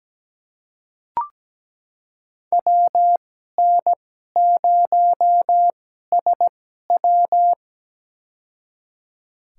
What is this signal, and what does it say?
Morse code, heard as a keyed tone, about 17 wpm, spaced as for 12 wpm, 700 Hz. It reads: WN0SW
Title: USA Call Signs - 5 Characters - 2x Character Spacing - 17wpm